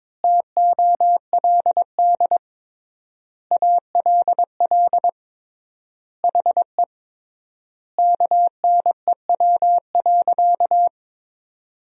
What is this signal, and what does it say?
Morse code, heard as a keyed tone, about 21 wpm, 700 Hz. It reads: TOLD ALL HE KNEW.
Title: 3x Word-Spacing - Sentences from Top 400 Words - 22wpm